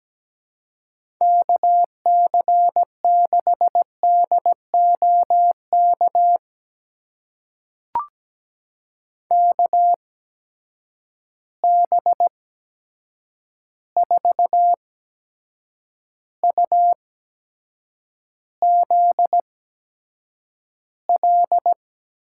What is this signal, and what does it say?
Morse code, heard as a keyed tone, about 17 wpm, spaced as for 7 wpm, 700 Hz. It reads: KC6DOK KB4UZL
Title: USA Call Signs - 6 Characters - 8x Character Spacing - 17wpm